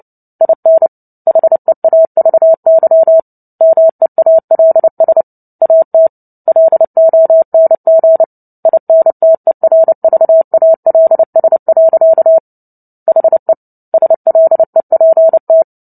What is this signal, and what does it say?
Morse code, heard as a keyed tone, about 30 wpm, 650 Hz. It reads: IN HEAVY MEALS AT LONG INTERVALS. HE SLEPT
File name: Book Two / 03-Chapter 3 - 29 WPM